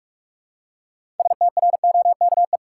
Morse code, heard as a keyed tone, about 46 wpm, 700 Hz. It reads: STROKE
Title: Six Letter Words - 45wpm